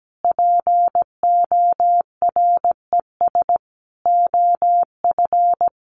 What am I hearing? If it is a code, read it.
Morse code, heard as a keyed tone, about 17 wpm, 700 Hz. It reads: PORES OF
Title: A Sound Of Thunder 17wpm